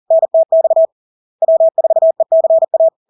Morse code, heard as a keyed tone, about 40 wpm, 650 Hz. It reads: NTX W4ECA
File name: LARK-FD-2022-Log-in-CW-40wpm-650-Hz